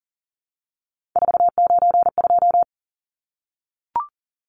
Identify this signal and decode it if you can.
Morse code, heard as a keyed tone, about 40 wpm, 700 Hz. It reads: 492
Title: Numbers - 3-Digit 40wpm